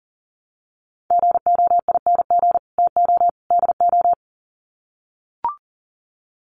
Morse code, heard as a keyed tone, about 40 wpm, 700 Hz. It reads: GOING TO DO
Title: Sets of 3 Words 40wpm